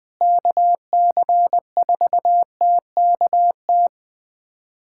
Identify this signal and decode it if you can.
Morse code, heard as a keyed tone, about 20 wpm, 700 Hz. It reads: KC4TKT
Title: USA Callsign DE Callsign - 6 Characters - 20wpm